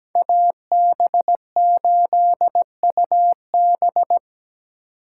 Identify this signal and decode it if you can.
Morse code, heard as a keyed tone, about 20 wpm, 700 Hz. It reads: AB8UB
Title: Speed Racing - USA Call Signs 17wpm